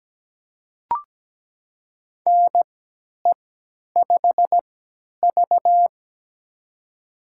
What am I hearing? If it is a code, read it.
Morse code, heard as a keyed tone, about 17 wpm, spaced as for 10 wpm, 700 Hz. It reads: NE5V
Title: USA Call Signs - 4 Characters - 3x Character Spacing - 17wpm